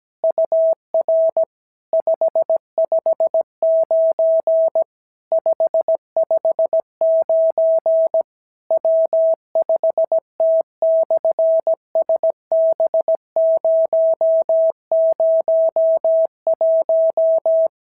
Morse code, heard as a keyed tone, about 17 wpm, 650 Hz. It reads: UR 559 559 W5T/SB001
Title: SOTA QSO Practice 17wpm